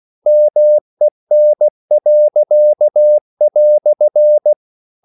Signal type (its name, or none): Morse code